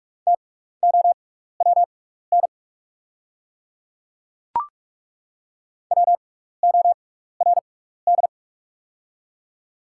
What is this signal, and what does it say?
Morse code, heard as a keyed tone, about 42 wpm, spaced as for 15 wpm, 700 Hz. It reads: TOWN WORD
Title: 6x Character Spacing - Top 500 Words - 45wpm